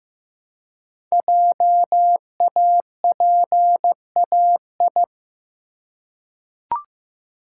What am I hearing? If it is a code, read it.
Morse code, heard as a keyed tone, about 15 wpm, 700 Hz. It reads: JAPAI
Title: Code Groups - 5 Letter:Numbers - 15wpm